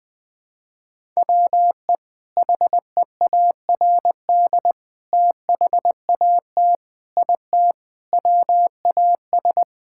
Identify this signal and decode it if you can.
Morse code, heard as a keyed tone, about 20 wpm, 700 Hz. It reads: WE HEARD THAT IT WAS
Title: Sentences from Top 1000 Words - Review Part 6 - 20wpm